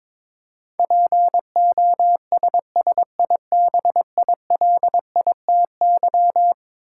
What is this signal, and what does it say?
Morse code, heard as a keyed tone, about 22 wpm, 700 Hz. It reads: POSSIBILITY
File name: practice / 22wpm / Eleven Letter Words - 22wpm